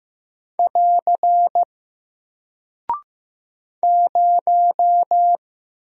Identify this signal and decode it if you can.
Morse code, heard as a keyed tone, about 15 wpm, 700 Hz. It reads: <AR> 0
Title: Single Character (Letter:Number:Punctuation:Prosign) 15wpm